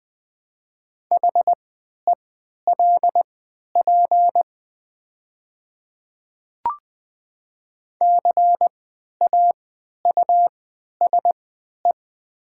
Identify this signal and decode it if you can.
Morse code, heard as a keyed tone, about 20 wpm, spaced as for 11 wpm, 700 Hz. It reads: HELP CAUSE
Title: 3x Character Spacing - Top 200 Words - 20wpm